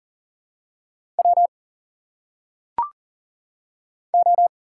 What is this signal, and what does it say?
Morse code, heard as a keyed tone, about 39 wpm, 700 Hz. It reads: W O
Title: Single Letters - Phonetic Pronunciation 40wpm